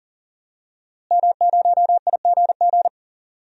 Morse code, heard as a keyed tone, about 40 wpm, 700 Hz. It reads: M0IGG
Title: DXCC-100 Call Signs - 5 Characters - Repeated 2x - 40wpm